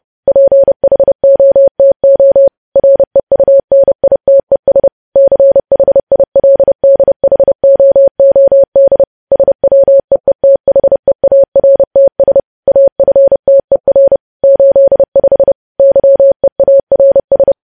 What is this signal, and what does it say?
Morse code, heard as a keyed tone, about 30 wpm, 550 Hz. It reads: PHOTO REUNITES CHILDHOOD SWEETHEARTS AFTER 85 YEARS
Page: news headlines in morse code at 30 wpm